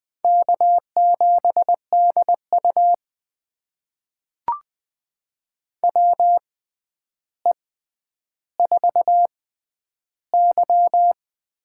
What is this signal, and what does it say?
Morse code, heard as a keyed tone, about 20 wpm, spaced as for 10 wpm, 700 Hz. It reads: K7DU WE4Y
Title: USA Call Signs - 6x Character Spacing - 20wpm